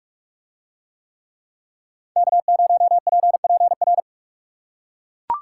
Morse code, heard as a keyed tone, about 44 wpm, 700 Hz. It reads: K0PPR